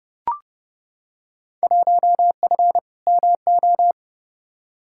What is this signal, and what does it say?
Morse code, heard as a keyed tone, about 30 wpm, 700 Hz. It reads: 1F MO
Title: Field Day Exchanges Encore - 30wpm